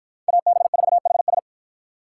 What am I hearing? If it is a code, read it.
Morse code, heard as a keyed tone, about 53 wpm, 700 Hz. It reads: AB4HS